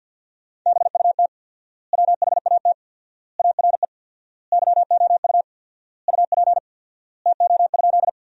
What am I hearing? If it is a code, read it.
Morse code, heard as a keyed tone, about 47 wpm, 700 Hz. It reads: BUT WHAT ARE YOU UP TO?
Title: 4x Word-Spacing - Sentences from Top 100 Words - 50wpm